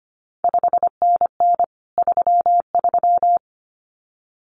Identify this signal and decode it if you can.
Morse code, heard as a keyed tone, about 25 wpm, 700 Hz. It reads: 5NN 33
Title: CQ WW DX Contest Exchange - 25wpm